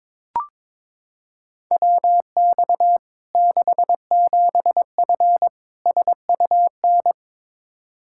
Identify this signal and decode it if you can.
Morse code, heard as a keyed tone, about 22 wpm, 700 Hz. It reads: WX 67F SUN
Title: K1USN SST QSO Practice 22wpm